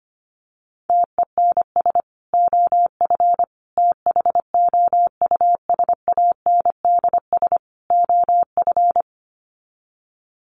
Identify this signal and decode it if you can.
Morse code, heard as a keyed tone, about 25 wpm, 700 Hz. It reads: TENS OF THOUSANDS OF
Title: Sets of 4 Words 25wpm